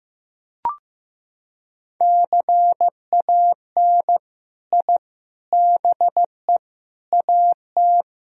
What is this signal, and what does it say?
Morse code, heard as a keyed tone, about 15 wpm, 700 Hz. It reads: CAN I BE AT
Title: Sentences from Top 1000 Words - Review Part 1 - 15wpm